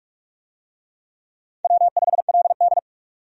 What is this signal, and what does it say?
Morse code, heard as a keyed tone, about 47 wpm, 700 Hz. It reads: W5LD
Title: USA Call Signs - 4 Characters - 45wpm